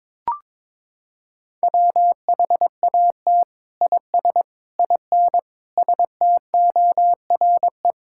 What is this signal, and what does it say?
Morse code, heard as a keyed tone, about 22 wpm, 700 Hz. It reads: WHAT IS IN STORE
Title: Sentences from Top 1000 Words - Review Part 5 - 22wpm